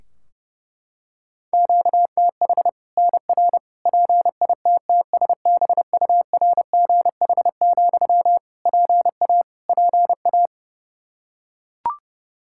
Morse code, heard as a keyed tone, about 30 wpm, 700 Hz. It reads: QTH NR PITTSBURGH, PA PA
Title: Practice QSO - Exchange 1 - Beginning - 30wpm